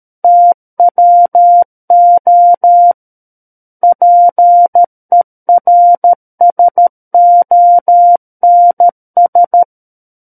Morse code, heard as a keyed tone, about 13 wpm, 700 Hz. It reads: TWO PERSONS
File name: Book One / 04-Chapter 4 - 13 WPM